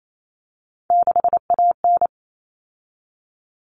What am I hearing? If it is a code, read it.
Morse code, heard as a keyed tone, about 28 wpm, 700 Hz. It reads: 6AN
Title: All Cut Numbers - 3-Digit 28wpm